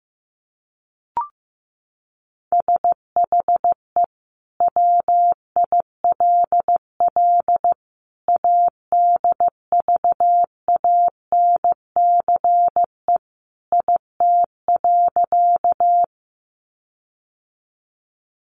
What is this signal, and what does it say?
Morse code, heard as a keyed tone, about 15 wpm, 700 Hz. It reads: SHE WILL ADVANCE IT.
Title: Sentences from Top 1400 Words 15wpm